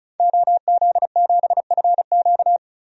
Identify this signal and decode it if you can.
Morse code, heard as a keyed tone, about 36 wpm, 700 Hz. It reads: OZ7FQ
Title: DXCC-100 Call Signs - 5 Characters - Repeated 2x - 35wpm